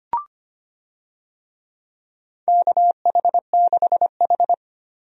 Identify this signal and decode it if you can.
Morse code, heard as a keyed tone, about 25 wpm, 700 Hz. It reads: KH6H
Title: USA Call Signs - 4 Characters - 25wpm